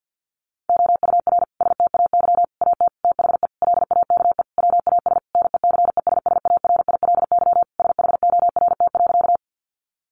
Morse code, heard as a keyed tone, about 50 wpm, 700 Hz. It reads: OUR STAY AT THE LAKE WAS NECESSARILY SHORT.
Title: Sentences from Top 1400 Words 50wpm